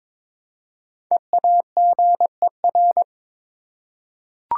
Morse code, heard as a keyed tone, about 22 wpm, 690 Hz. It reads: EAGER
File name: 5 Letter Words - First 3 Letters Given - 22wpm